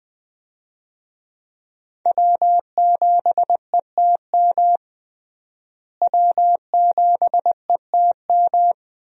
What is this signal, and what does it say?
Morse code, heard as a keyed tone, about 20 wpm, 700 Hz. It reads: W7ETM W7ETM
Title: USA Call Signs - 5 Characters - Repeated 2x - 20wpm